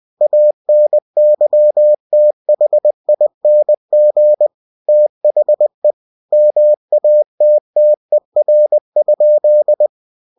Morse code, heard as a keyed tone, about 20 wpm, 600 Hz. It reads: ANYTHING THE MATTER?
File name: cw-books / winnie-the-pooh / winnie-20wpm-page-0046